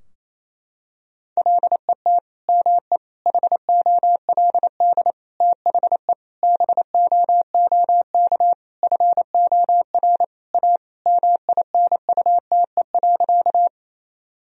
Morse code, heard as a keyed tone, about 28 wpm, 700 Hz. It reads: LET ME HOLD THE BOOK FOR A MINUTE.
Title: Sentences from Top 1000 Words - Review Part 2 - 28wpm